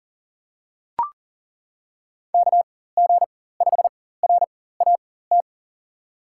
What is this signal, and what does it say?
Morse code, heard as a keyed tone, about 37 wpm, spaced as for 18 wpm, 700 Hz. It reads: KG5RAT
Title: USA Call Signs - 6 Characters - 4x Character Spacing - 40wpm